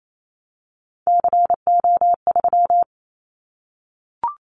Morse code, heard as a keyed tone, about 28 wpm, 700 Hz. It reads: CO3
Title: Callsign Prefixes for Top 100 DXCC - 28wpm